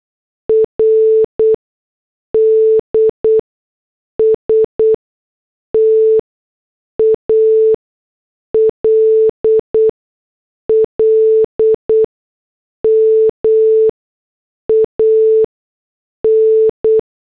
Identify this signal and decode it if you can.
Morse code, heard as a keyed tone, about 8 wpm, spaced as for 6 wpm, 440 Hz. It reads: RD STALLMAN
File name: morse / headlines